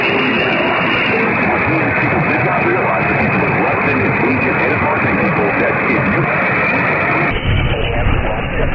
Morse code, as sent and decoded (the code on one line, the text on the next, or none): none